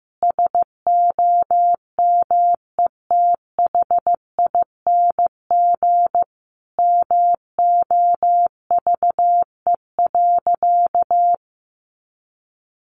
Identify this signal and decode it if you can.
Morse code, heard as a keyed tone, about 15 wpm, 700 Hz. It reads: SOMETHING MOVE.
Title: Sentences from Top 1200 Words 15wpm